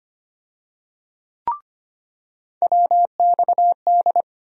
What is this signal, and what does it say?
Morse code, heard as a keyed tone, about 25 wpm, 700 Hz. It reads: WXD
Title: USA Call Sign Postfixes - 3 Letters 25wpm